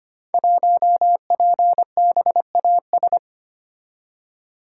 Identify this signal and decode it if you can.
Morse code, heard as a keyed tone, about 25 wpm, 700 Hz. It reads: 1PBAS
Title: Code Groups - 5 Letter:Numbers - 25wpm